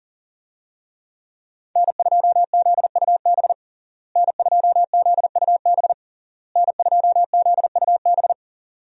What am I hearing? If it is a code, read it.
Morse code, heard as a keyed tone, about 40 wpm, 700 Hz. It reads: N2ZUB N2ZUB N2ZUB
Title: USA Call Signs - 5 Characters - Repeated 3x - 40wpm